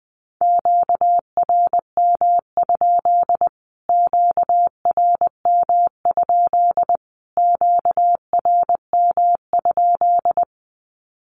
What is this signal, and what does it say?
Morse code, heard as a keyed tone, about 20 wpm, 700 Hz. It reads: QRM? QRM? QRM?
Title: Q-Codes Repeated 3x 20wpm